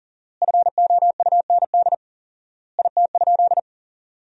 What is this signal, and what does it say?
Morse code, heard as a keyed tone, about 40 wpm, 700 Hz. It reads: FOUND IT?
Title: 4x Word-Spacing - Sentences from Top 300 Words - 40wpm